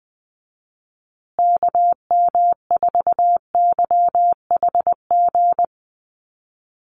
Morse code, heard as a keyed tone, about 20 wpm, 700 Hz. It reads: KM4YHG
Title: USA Call Signs Repeated 2x 20wpm